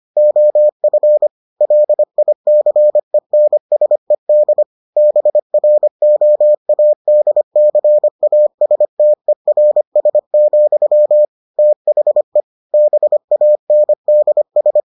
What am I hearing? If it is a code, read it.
Morse code, heard as a keyed tone, about 25 wpm, 600 Hz. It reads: OF LICENSED BROADCASTERS, THE BANDS